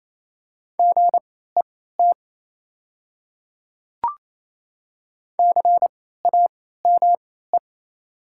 Morse code, heard as a keyed tone, about 28 wpm, spaced as for 16 wpm, 700 Hz. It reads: GET CAME
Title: 3x Character Spacing - Top 200 Words - 28wpm